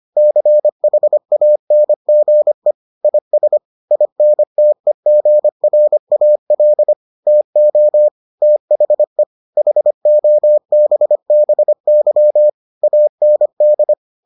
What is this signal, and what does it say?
Morse code, heard as a keyed tone, about 25 wpm, 600 Hz. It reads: CHANGE IS INTEGRAL TO THE HOBBY AND